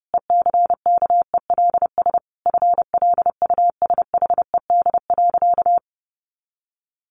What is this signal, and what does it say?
Morse code, heard as a keyed tone, about 30 wpm, 700 Hz. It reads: ECKELS FLUSHED.
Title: A Sound Of Thunder 30wpm